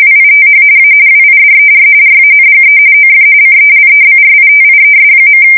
RTTY, about 45.45 baud. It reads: WITH A WIDER RANGE OF CELEBRITIES.
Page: 170 Shift 2125/2295